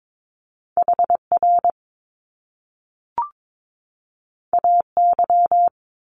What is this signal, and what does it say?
Morse code, heard as a keyed tone, about 22 wpm, 700 Hz. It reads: HR AY
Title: Top Two Letter Combinations in Words 22wpm